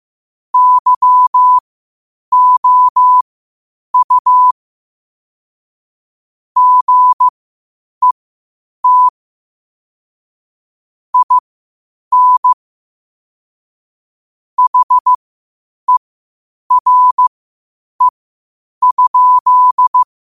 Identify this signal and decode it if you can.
Morse code, heard as a keyed tone, about 15 wpm, spaced as for 8 wpm, 1000 Hz. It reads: YOU GET IN HERE?
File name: qotdfiles / 8wpm